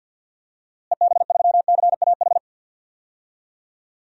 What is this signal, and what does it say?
Morse code, heard as a keyed tone, about 47 wpm, 700 Hz. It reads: EB3CAH
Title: DXCC-100 Call Signs - 6 Characters - Repeated 2x - 50wpm